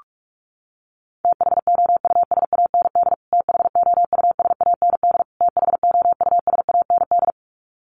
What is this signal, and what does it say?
Morse code, heard as a keyed tone, about 44 wpm, 700 Hz. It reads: THOUSAND THOUSAND THOUSAND
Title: Top 500 Words Repeated 45wpm